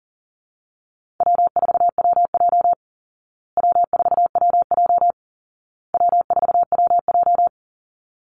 Morse code, heard as a keyed tone, about 40 wpm, 700 Hz. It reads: W4WJ W4WJ W4WJ
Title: Call Signs Repeated 2x - 40wpm